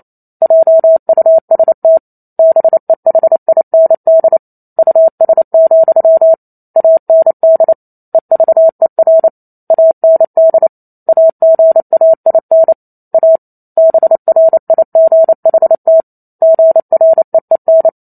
Morse code, heard as a keyed tone, about 28 wpm, 650 Hz. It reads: JUST BEHIND US, AND EVER AND AGAIN A BRIGHT GREEN
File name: Chapter 2 - 28 WPM